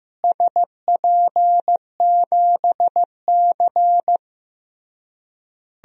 Morse code, heard as a keyed tone, about 15 wpm, 700 Hz. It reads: SP7C
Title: DXCC-100 Call Signs - 4 Characters - Repeated 3x - 15wpm